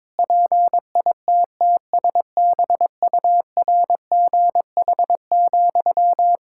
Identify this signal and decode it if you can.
Morse code, heard as a keyed tone, about 22 wpm, 700 Hz. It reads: PITTSBURGH,
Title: Practice QSO - Exchange 1 - Beginning - 22wpm